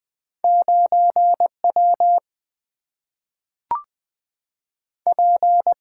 Morse code, heard as a keyed tone, about 20 wpm, 700 Hz. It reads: 9W P
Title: Single-Letter Single-Number 20wpm